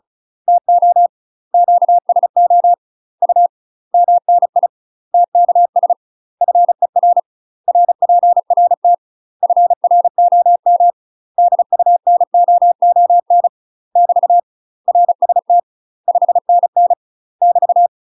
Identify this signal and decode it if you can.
Morse code, heard as a keyed tone, about 36 wpm, 700 Hz. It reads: TO QSO U MNI TKS FER RPRT FROM DUNOON <BT> RST 5NN <BT>
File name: UK MASTER QSOs TRAINING / 35WPM / UK MASTER QSOs 0026